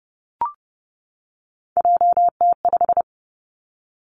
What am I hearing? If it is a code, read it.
Morse code, heard as a keyed tone, about 30 wpm, 700 Hz. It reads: JT5
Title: Three Challenging Letter-Number Combinations 30wpm